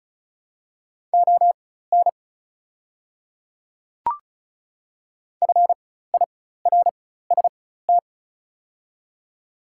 Morse code, heard as a keyed tone, about 34 wpm, spaced as for 16 wpm, 700 Hz. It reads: ON FIRST
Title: 4x Character Spacing - Top 100 Words - 35wpm